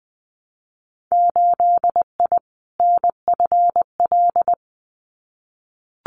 Morse code, heard as a keyed tone, about 19 wpm, 700 Hz. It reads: 8I NFL
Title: Winter Field Day Exchanges Encore 20wpm